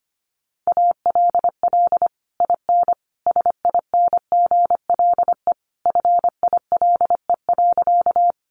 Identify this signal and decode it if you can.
Morse code, heard as a keyed tone, about 25 wpm, 700 Hz. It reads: ALL IN SINGLE FILE.
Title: Sentences from Top 1200 Words 25wpm